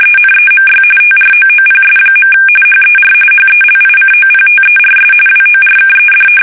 RTTY, about 45.45 baud. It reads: ENEMY TROOPS. HER HELICOPTERS WORKED
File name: fsk850 / battleshipnj